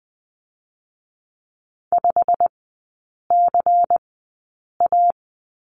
Morse code, heard as a keyed tone, about 20 wpm, 700 Hz. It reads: 5 C A
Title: 114 Lesson 19e - Letter w - Three Characters 20wpm